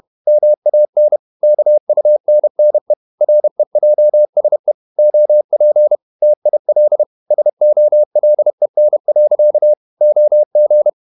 Morse code, heard as a keyed tone, about 32 wpm, 600 Hz. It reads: MAN KUNNE REJSE OP TIL SOLEN. OG
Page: Ferie 30 wpm